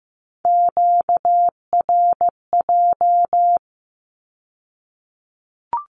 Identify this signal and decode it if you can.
Morse code, heard as a keyed tone, about 15 wpm, 700 Hz. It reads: QRJ
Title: Q-Codes with Spoken Meaning - 15wpm